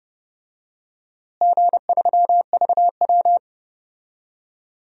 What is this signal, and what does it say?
Morse code, heard as a keyed tone, about 32 wpm, 700 Hz. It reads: G3VW
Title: DXCC-100 Call Signs - 4 Characters - Repeated 2x - 30wpm